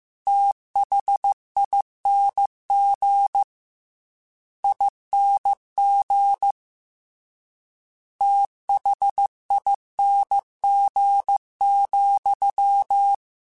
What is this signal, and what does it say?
Morse code, heard as a keyed tone, about 15 wpm, 800 Hz. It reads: THING ING THING,
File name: Special INGs-mp3s / Special INGs-mp3s / VerySlowToFaster Ing @13